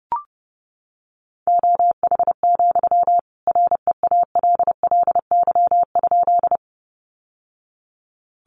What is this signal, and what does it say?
Morse code, heard as a keyed tone, about 30 wpm, 700 Hz. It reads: OH, REALLY?
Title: Sentences from Top 600 Words 30wpm